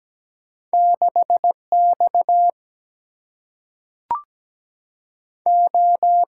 Morse code, heard as a keyed tone, about 17 wpm, 700 Hz. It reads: 6 X O
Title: Single-Letter Single-Number 17wpm